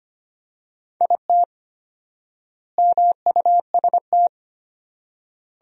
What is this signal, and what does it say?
Morse code, heard as a keyed tone, about 25 wpm, 700 Hz. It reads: IT MUST
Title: 4x Word-Spacing - Sentences from Top 500 Words - 25wpm